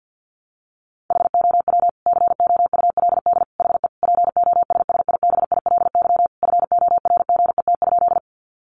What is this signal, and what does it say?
Morse code, heard as a keyed tone, about 50 wpm, 700 Hz. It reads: HOW COULD HE POSSIBILY FORGET?
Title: Sentences from Top 1300 Words 50wpm